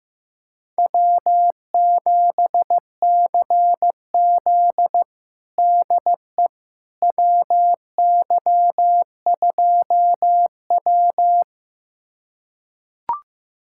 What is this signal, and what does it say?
Morse code, heard as a keyed tone, about 15 wpm, 700 Hz. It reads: W7CZ DE WY2W
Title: USA Callsign DE Callsign - 4 Characters - 15wpm